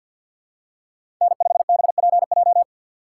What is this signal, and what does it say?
Morse code, heard as a keyed tone, about 52 wpm, 700 Hz. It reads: N5BPJ
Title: USA Call Signs - 4x Character Spacing - 50wpm